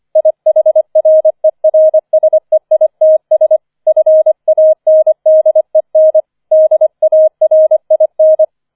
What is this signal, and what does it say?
Morse code, heard as a keyed tone, about 25 wpm, 630 Hz. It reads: IHRERSEITS FANDEN DARIN
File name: 005 Fünfter Streich 25wpm